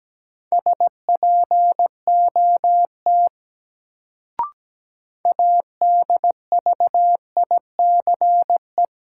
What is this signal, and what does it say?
Morse code, heard as a keyed tone, about 17 wpm, 700 Hz. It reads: SPOT ADVICE
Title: Sentences from Top 1100 Words - New Word Drill - 17wpm